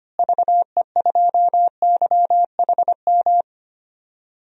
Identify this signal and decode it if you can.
Morse code, heard as a keyed tone, about 25 wpm, 700 Hz. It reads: VE2YHM
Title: DXCC-100 Call Signs - 6 Characters - Repeated 2x - 25wpm